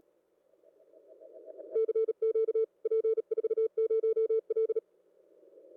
Morse code, heard as a keyed tone, about 37 wpm, 440 Hz. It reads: CQ P40L